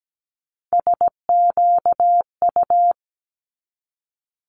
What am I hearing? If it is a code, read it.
Morse code, heard as a keyed tone, about 17 wpm, 700 Hz. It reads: SQU
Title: Multi-letter Phonemes 17wpm